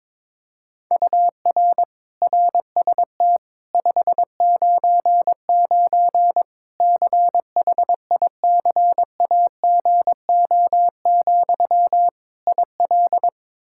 Morse code, heard as a keyed tone, about 22 wpm, 700 Hz. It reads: UR RST 599 CHICAGO, IL
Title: Practice QSO - Exchange 1 - Beginning - 22wpm